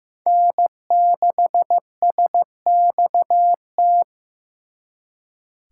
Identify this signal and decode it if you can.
Morse code, heard as a keyed tone, about 18 wpm, 700 Hz. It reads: N6SXT
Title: Speed Racing - USA Call Signs 15wpm